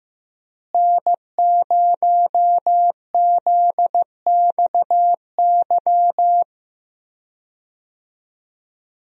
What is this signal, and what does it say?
Morse code, heard as a keyed tone, about 15 wpm, 700 Hz. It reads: N0ZXY
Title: USA Call Signs - 5 Characters - Repeated 3x - 15wpm